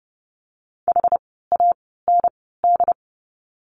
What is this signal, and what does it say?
Morse code, heard as a keyed tone, about 30 wpm, spaced as for 17 wpm, 700 Hz. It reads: HAND
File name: course / 30:10wpm / 115 Lesson 19f - Letter w - Words 30wpm